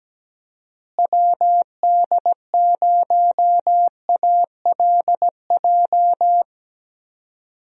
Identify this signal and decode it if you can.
Morse code, heard as a keyed tone, about 17 wpm, 700 Hz. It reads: WD0ALJ
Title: USA Call Signs - 6 Characters - Repeated 3x - 17wpm